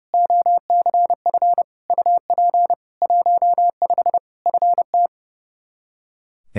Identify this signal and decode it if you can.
Morse code, heard as a keyed tone, about 30 wpm, 700 Hz. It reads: OCF UP 15 FT
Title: Practice QSO - Exchange 2 - Info - 30wpm